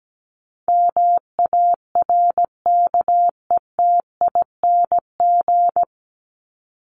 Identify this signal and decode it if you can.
Morse code, heard as a keyed tone, about 17 wpm, 700 Hz. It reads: MARKETING
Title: Top 751-1000 Words 17wpm